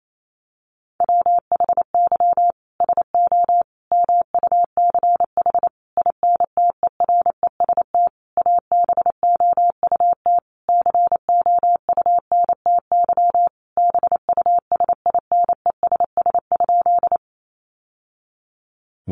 Morse code, heard as a keyed tone, about 28 wpm, 700 Hz. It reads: WHY SO MUCH INTEREST ABOUT COUNTY BUSINESS?
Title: Sentences from Top 1200 Words 28wpm